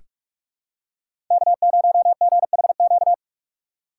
Morse code, heard as a keyed tone, about 46 wpm, 700 Hz. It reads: K0GHQ
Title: USA Call Signs - 5 Characters - 4x Character Spacing - 45wpm